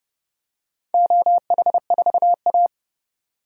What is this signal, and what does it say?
Morse code, heard as a keyed tone, about 30 wpm, 700 Hz. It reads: OH4A
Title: DXCC-100 Call Signs - 4 Characters - 30wpm